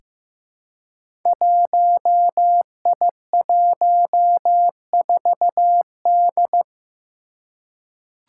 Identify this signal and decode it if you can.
Morse code, heard as a keyed tone, about 15 wpm, 700 Hz. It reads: 1I14D
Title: Five Challenging Letter-Number Combinations 15wpm